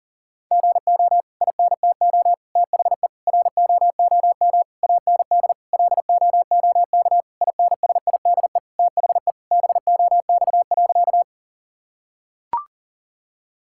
Morse code, heard as a keyed tone, about 40 wpm, 700 Hz. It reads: GO INTO THE ROOM AND LOOK INSIDE THE BOX.
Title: Sentences from Top 1000 Words - Review Part 6 - 40wpm